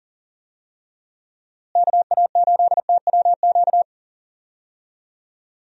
Morse code, heard as a keyed tone, about 38 wpm, 700 Hz. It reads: KA8TWQ
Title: USA Call Signs - 6 Characters - Repeated 3x - 40wpm